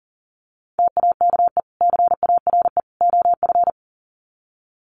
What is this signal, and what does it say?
Morse code, heard as a keyed tone, about 40 wpm, 700 Hz. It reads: TAKE CARE OF
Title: Sets of 3 Words 40wpm